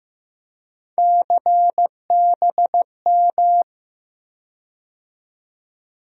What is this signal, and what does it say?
Morse code, heard as a keyed tone, about 15 wpm, 700 Hz. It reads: CBM
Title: USA Call Sign Postfixes - 3 Letters 15wpm